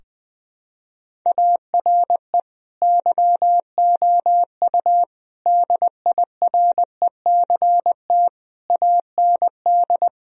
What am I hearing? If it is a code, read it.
Morse code, heard as a keyed tone, about 20 wpm, 700 Hz. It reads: ARE YOU DIRECT AND
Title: Sentences from Top 1000 Words - Review Part 4 - 20wpm